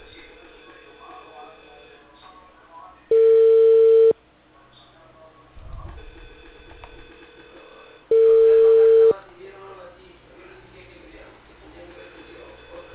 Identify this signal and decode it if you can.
Ringback tone (China, 450 Hz), with speech.